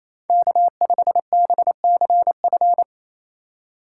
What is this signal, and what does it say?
Morse code, heard as a keyed tone, about 34 wpm, 700 Hz. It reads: K5BCF
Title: Speed Racing - USA Call Signs 28wpm